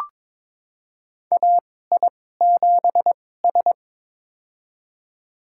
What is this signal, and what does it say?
Morse code, heard as a keyed tone, about 22 wpm, spaced as for 16 wpm, 700 Hz. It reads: AI7S
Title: USA Call Signs - 4 Characters - 2x Character Spacing - 22wpm